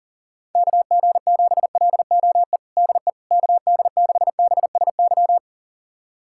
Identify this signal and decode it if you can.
Morse code, heard as a keyed tone, about 37 wpm, 700 Hz. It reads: KG7LOE DE KD6BSY
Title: USA Callsign DE Callsign - 6 Characters - 40wpm